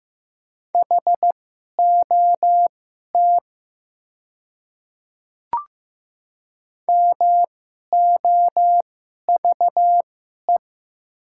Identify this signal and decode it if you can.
Morse code, heard as a keyed tone, about 15 wpm, spaced as for 11 wpm, 700 Hz. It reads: HOT MOVE
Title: 2x Character Spacing - Top 500 Words - 15wpm